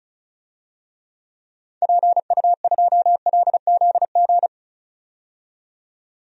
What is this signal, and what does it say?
Morse code, heard as a keyed tone, about 36 wpm, 700 Hz. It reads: PU2LZG
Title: DXCC-100 Call Signs - 6 Characters - 35wpm